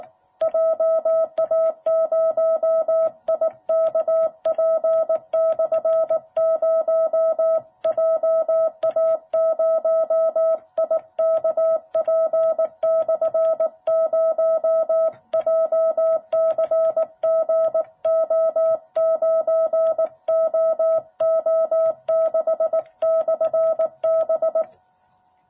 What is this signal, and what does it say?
Morse code, heard as a keyed tone, about 18 wpm, 660 Hz. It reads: JA0IKP/0 JA0IKP/0 JCGO9OO6/B
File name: jh8cbh / 2021cw